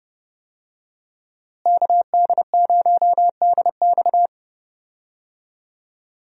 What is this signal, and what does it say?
Morse code, heard as a keyed tone, about 30 wpm, 700 Hz. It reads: KD0DX
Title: USA Call Signs - 5 Characters - Repeated 2x - 30wpm